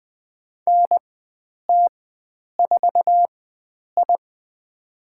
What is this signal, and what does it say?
Morse code, heard as a keyed tone, about 20 wpm, spaced as for 9 wpm, 700 Hz. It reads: NT4I
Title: USA Call Signs - 4 Characters - 4x Character Spacing - 20wpm